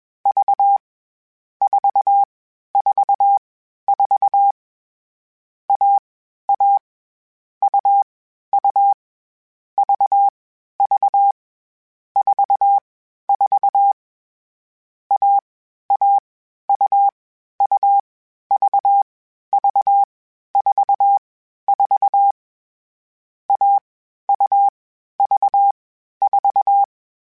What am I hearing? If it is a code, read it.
Morse code, heard as a keyed tone, about 21 wpm, 800 Hz. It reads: V 4 4 4 A A U U V V 4 4 A A U U V V 4 4 A U V 4
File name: Difficult Combos / Difficult Combos-mp3s / AUV AUV4 @20